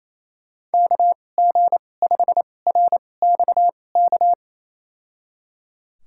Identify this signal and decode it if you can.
Morse code, heard as a keyed tone, about 28 wpm, spaced as for 20 wpm, 700 Hz. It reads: KG5RXK
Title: USA Call Signs - 6 Characters - 2x Character Spacing - 28wpm